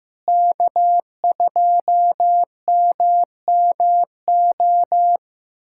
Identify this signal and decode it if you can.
Morse code, heard as a keyed tone, about 15 wpm, 700 Hz. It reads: K2MMO
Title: K1USN SST QSO Practice - 15wpm